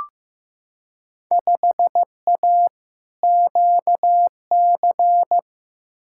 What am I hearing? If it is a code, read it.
Morse code, heard as a keyed tone, about 15 wpm, 700 Hz. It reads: 5A QC
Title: Field Day Exchanges Encore - 15wpm